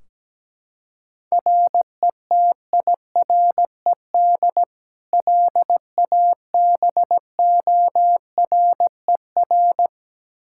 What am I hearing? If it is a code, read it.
Morse code, heard as a keyed tone, about 17 wpm, 700 Hz. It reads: RETIRED LABORER
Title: K1USN SST QSO Practice 17wpm